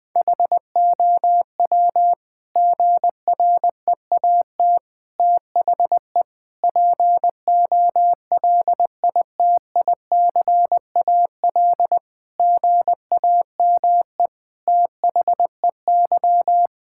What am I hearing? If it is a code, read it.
Morse code, heard as a keyed tone, about 20 wpm, 700 Hz. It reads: HOW GREAT THE POLITICAL GAME THEY
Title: Sentences from Top 1000 Words - Review Part 5 - 20wpm